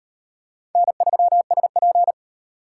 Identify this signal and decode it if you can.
Morse code, heard as a keyed tone, about 38 wpm, 700 Hz. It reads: N3SP